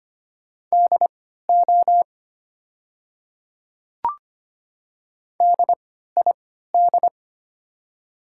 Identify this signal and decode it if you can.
Morse code, heard as a keyed tone, about 25 wpm, spaced as for 14 wpm, 700 Hz. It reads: DO DID
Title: 3x Character Spacing - Top 100 Words - 25wpm